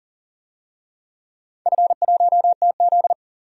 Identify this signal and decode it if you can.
Morse code, heard as a keyed tone, about 40 wpm, 700 Hz. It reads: F1TZ
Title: DXCC-100 Call Signs - 4 Characters - 40wpm